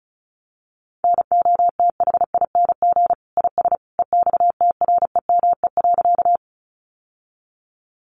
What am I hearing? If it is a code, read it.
Morse code, heard as a keyed tone, about 34 wpm, 700 Hz. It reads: NOTHING IS EXTREME.
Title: Sentences from Top 1300 Words 35wpm